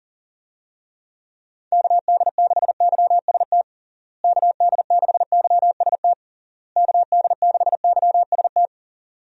Morse code, heard as a keyed tone, about 40 wpm, 700 Hz. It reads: KD6YST KD6YST KD6YST
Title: USA Call Signs - 6 Characters - Repeated 3x - 40wpm